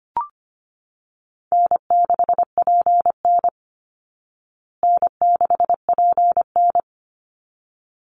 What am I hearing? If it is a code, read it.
Morse code, heard as a keyed tone, about 25 wpm, 700 Hz. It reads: N6PN N6PN
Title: Easy Call Signs Repeated 2x - 25wpm